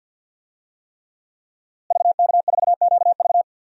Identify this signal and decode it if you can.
Morse code, heard as a keyed tone, about 50 wpm, 700 Hz. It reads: VK4QV
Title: DXCC-100 Call Signs - 5 Characters - Repeated 3x - 50wpm